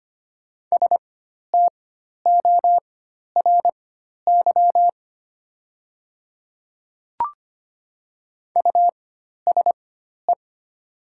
Morse code, heard as a keyed tone, about 25 wpm, spaced as for 12 wpm, 700 Hz. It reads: STORY USE
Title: 4x Character Spacing - Top 300 Words - 25wpm